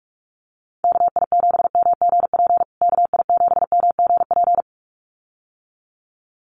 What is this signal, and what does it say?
Morse code, heard as a keyed tone, about 44 wpm, 700 Hz. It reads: KI7MGP KI7MGP
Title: USA Call Signs Repeated 2x 45wpm